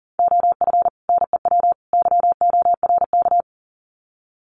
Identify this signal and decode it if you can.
Morse code, heard as a keyed tone, about 40 wpm, 700 Hz. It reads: OF NEW YORK
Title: Sets of 3 Words 40wpm